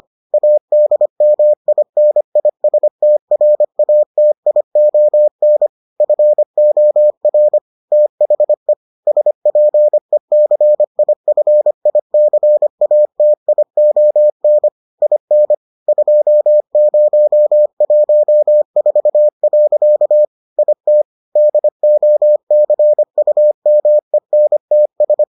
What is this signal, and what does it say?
Morse code, heard as a keyed tone, about 25 wpm, 600 Hz. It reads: ADMINISTRATION FOR THE SPECIFICATION IN 2014. IT DOCUMENTS